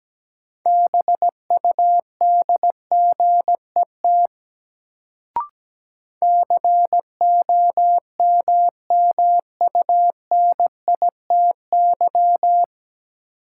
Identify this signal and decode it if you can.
Morse code, heard as a keyed tone, about 17 wpm, 700 Hz. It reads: BUDGET COMMUNITY
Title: Sentences from Top 1200 Words - New Word Drill Alt - 17wpm